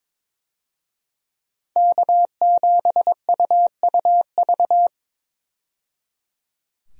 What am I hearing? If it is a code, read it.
Morse code, heard as a keyed tone, about 22 wpm, 700 Hz. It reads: K7UUV
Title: USA Call Signs - 5 Characters - 22wpm